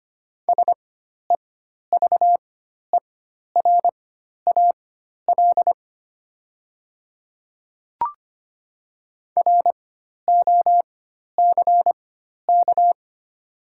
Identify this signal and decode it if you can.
Morse code, heard as a keyed tone, about 25 wpm, spaced as for 12 wpm, 690 Hz. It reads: SEVERAL ROCK
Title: 4x Character Spacing - Top 500 Words - 25wpm